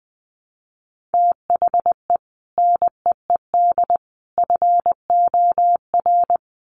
Morse code, heard as a keyed tone, about 20 wpm, 700 Hz. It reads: THE NEED FOR
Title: Sets of 3 Words 20wpm